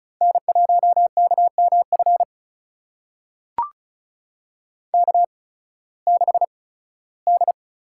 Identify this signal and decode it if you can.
Morse code, heard as a keyed tone, about 34 wpm, spaced as for 18 wpm, 700 Hz. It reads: N1KMF K6D
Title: USA Call Signs - 8x Character Spacing - 35wpm